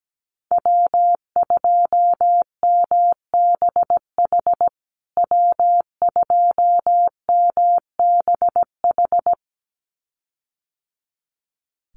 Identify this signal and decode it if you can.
Morse code, heard as a keyed tone, about 17 wpm, 700 Hz. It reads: W2MBH W2MBH
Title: USA Call Signs Repeated 2x 17wpm